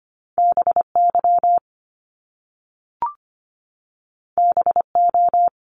Morse code, heard as a keyed tone, about 25 wpm, 700 Hz. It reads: BY BO
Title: Top Two Letter Combinations in Words 25wpm